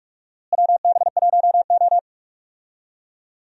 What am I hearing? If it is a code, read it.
Morse code, heard as a keyed tone, about 42 wpm, 700 Hz. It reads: WB1O
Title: USA Call Signs - 4 Characters - Repeated 3x - 45wpm